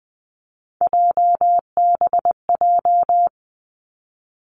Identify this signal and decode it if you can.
Morse code, heard as a keyed tone, about 20 wpm, 700 Hz. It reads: JBJ
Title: Three Challenging Letter-Number Combinations 20wpm